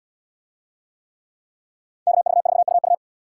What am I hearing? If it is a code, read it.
Morse code, heard as a keyed tone, about 77 wpm, 700 Hz. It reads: KV4UU